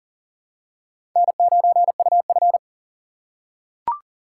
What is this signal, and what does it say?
Morse code, heard as a keyed tone, about 40 wpm, 700 Hz. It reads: N9UF
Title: USA Call Signs - 6x Character Spacing - 40wpm